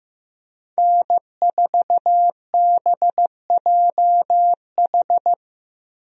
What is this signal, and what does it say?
Morse code, heard as a keyed tone, about 15 wpm, 700 Hz. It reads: N4BJH
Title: USA Call Signs - 5 Characters - Repeated 3x - 15wpm